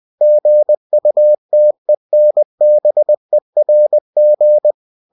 Morse code, heard as a keyed tone, about 20 wpm, 600 Hz. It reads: GUTENBERG
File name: cw-books / winnie-the-pooh / winnie-20wpm-page-0120